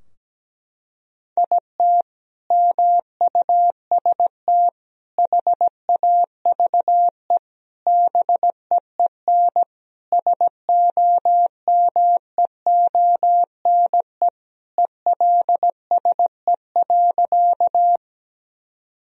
Morse code, heard as a keyed tone, about 17 wpm, 700 Hz. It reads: IT MUST HAVE BEEN SOMEONE ELSE.
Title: Sentences from Top 1000 Words - Review Part 4 - 17wpm